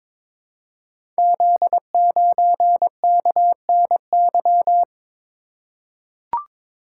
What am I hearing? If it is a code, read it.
Morse code, heard as a keyed tone, about 23 wpm, 690 Hz. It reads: Z9KNY
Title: Code Groups - 5 Letter:Numbers - 22wpm